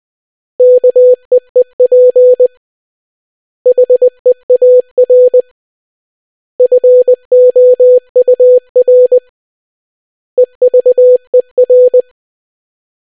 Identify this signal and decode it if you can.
Morse code, heard as a keyed tone, about 20 wpm, 500 Hz. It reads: KEEP HEAR FOUR EVER